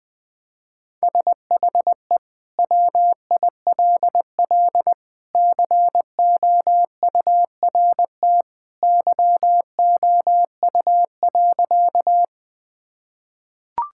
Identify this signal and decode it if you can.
Morse code, heard as a keyed tone, about 21 wpm, 700 Hz. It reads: SHE WILL COURT YOU.
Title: Sentences from Top 1600 Words 20wpm